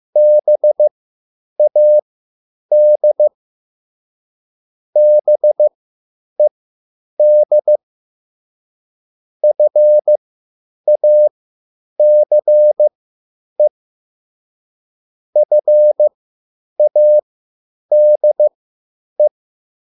Morse code, heard as a keyed tone, about 15 wpm, spaced as for 9 wpm, 600 Hz. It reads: BAD BED FACE FADE